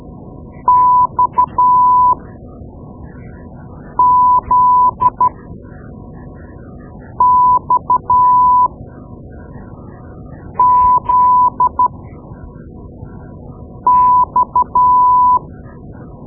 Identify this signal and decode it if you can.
Morse code, heard as a keyed tone, about 12 wpm, spaced as for 4 wpm, 1000 Hz. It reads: XZXZX
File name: Morse Lessons / Morse Lessons / Tape-2B